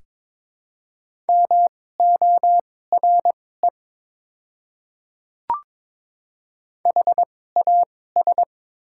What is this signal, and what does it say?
Morse code, heard as a keyed tone, about 22 wpm, spaced as for 16 wpm, 690 Hz. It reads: MORE HAS
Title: 2x Character Spacing - Top 100 Words - 22wpm